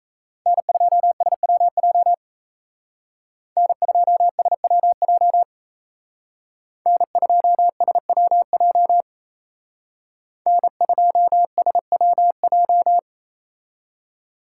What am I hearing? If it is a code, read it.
Morse code, heard as a keyed tone, about 35 wpm, 700 Hz. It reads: N2SWJ N2SWJ N2SWJ N2SWJ